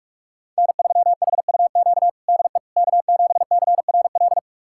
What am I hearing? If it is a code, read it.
Morse code, heard as a keyed tone, about 46 wpm, 700 Hz. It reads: N3HUQ DE K7CRL
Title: USA Callsign DE Callsign - 5 Characters - 45wpm